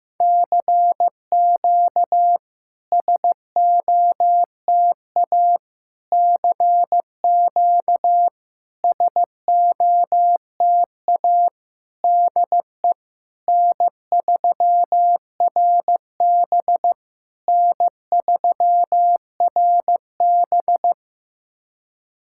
Morse code, heard as a keyed tone, about 15 wpm, 700 Hz. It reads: CQ SOTA CQ SOTA DE N3RB N3RB
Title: SOTA QSO Practice 15wpm